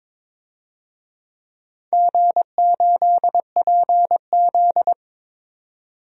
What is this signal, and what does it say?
Morse code, heard as a keyed tone, about 22 wpm, 700 Hz. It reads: G8PZ
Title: DXCC-100 Call Signs - 4 Characters - 22wpm